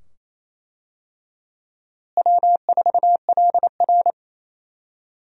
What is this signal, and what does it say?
Morse code, heard as a keyed tone, about 27 wpm, 700 Hz. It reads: W4LR
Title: USA Call Signs - 4 Characters - Repeated 2x - 28wpm